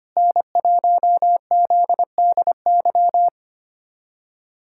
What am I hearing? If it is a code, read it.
Morse code, heard as a keyed tone, about 25 wpm, 700 Hz. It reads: N1ZDY
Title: USA Callsign DE Callsign - 5 Characters - 25wpm